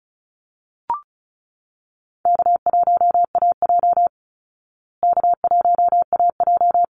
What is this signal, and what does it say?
Morse code, heard as a keyed tone, about 34 wpm, 700 Hz. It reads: K1AJ K1AJ
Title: Call Signs Repeated 3x - 35wpm